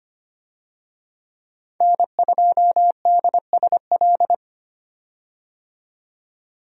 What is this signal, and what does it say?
Morse code, heard as a keyed tone, about 25 wpm, 700 Hz. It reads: N2DSL
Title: USA Call Signs - 5 Characters - 25wpm